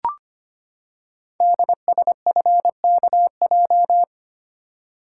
Morse code, heard as a keyed tone, about 25 wpm, 700 Hz. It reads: DSFKJ
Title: Code Groups - 5 Letters - 25wpm